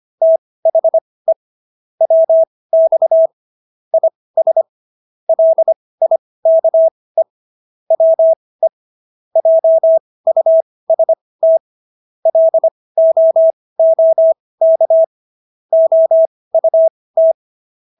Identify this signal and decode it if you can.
Morse code, heard as a keyed tone, about 25 wpm, spaced as for 18 wpm, 650 Hz. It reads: THE WX IS LIKE WE JUST LOOK OUT